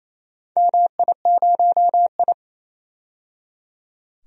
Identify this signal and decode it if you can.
Morse code, heard as a keyed tone, about 29 wpm, 700 Hz. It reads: MI0I
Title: DXCC-100 Call Signs - 4 Characters - 28wpm